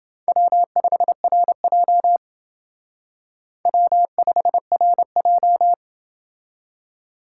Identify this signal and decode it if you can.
Morse code, heard as a keyed tone, about 29 wpm, 700 Hz. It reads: W5RJ W5RJ